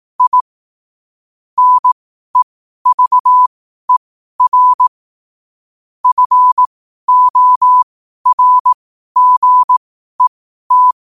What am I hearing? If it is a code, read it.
Morse code, heard as a keyed tone, about 18 wpm, spaced as for 12 wpm, 1000 Hz. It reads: I NEVER FORGET
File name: qotdfiles / 12wpm